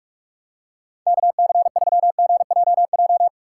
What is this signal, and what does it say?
Morse code, heard as a keyed tone, about 44 wpm, 700 Hz. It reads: KC3GJJ
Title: USA Call Signs - 6 Characters - 6x Character Spacing - 45wpm